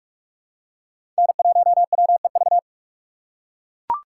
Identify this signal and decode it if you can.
Morse code, heard as a keyed tone, about 43 wpm, 700 Hz. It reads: N1WEV
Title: USA Call Signs - 5 Characters - 4x Character Spacing - 45wpm